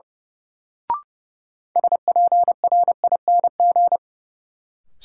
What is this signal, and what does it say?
Morse code, heard as a keyed tone, about 29 wpm, 700 Hz. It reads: SPRING
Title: Sentences from Top 1200 Words - New Word Drill Alt - 30wpm